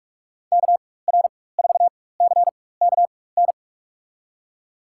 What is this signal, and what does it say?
Morse code, heard as a keyed tone, about 43 wpm, spaced as for 21 wpm, 700 Hz. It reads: KR4CKN
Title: USA Call Signs - 6 Characters - 4x Character Spacing - 45wpm